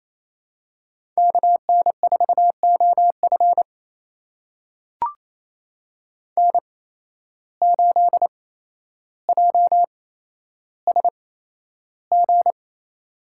Morse code, heard as a keyed tone, about 28 wpm, spaced as for 12 wpm, 700 Hz. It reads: KN4OF N8JSG
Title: USA Call Signs - 5 Characters - 8x Character Spacing - 28wpm